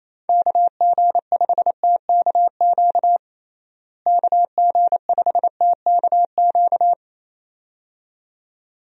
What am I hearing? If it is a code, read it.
Morse code, heard as a keyed tone, about 28 wpm, 700 Hz. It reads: KG5TKQ KG5TKQ
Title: USA Call Signs - 6 Characters - Repeated 2x - 28wpm